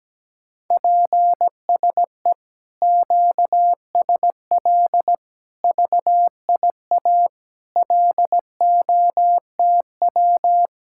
Morse code, heard as a keyed tone, about 17 wpm, 700 Hz. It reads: PSE QSL VIA LOTW
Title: Speed-Racing Common QSO Phrases - 17wpm